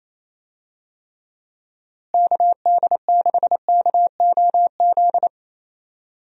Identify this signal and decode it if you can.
Morse code, heard as a keyed tone, about 28 wpm, 700 Hz. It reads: KD6KOZ
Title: USA Call Signs - 6 Characters - Repeated 2x - 28wpm